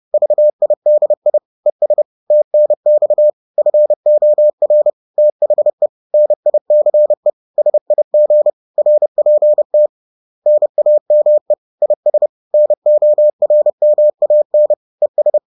Morse code, heard as a keyed tone, about 30 wpm, 600 Hz. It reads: VIDI ES TNX FOR THE NICE SIG RPT NAME IS NORMAN ES